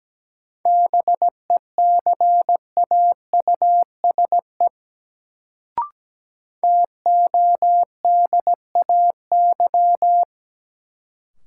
Morse code, heard as a keyed tone, about 17 wpm, 700 Hz. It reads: BECAUSE TODAY
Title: Sentences from Top 600 Words - New Word Drill - 17wpm